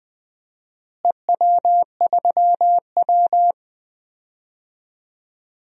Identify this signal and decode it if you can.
Morse code, heard as a keyed tone, about 20 wpm, 700 Hz. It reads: EW3W
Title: DXCC-100 Call Signs - 4 Characters - Repeated 2x - 20wpm